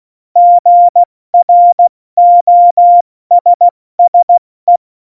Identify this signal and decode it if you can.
Morse code, heard as a keyed tone, about 16 wpm, 700 Hz. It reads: GROSSE
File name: Morsekurs / Witze / Witz 26. Januar 2021, 70 F80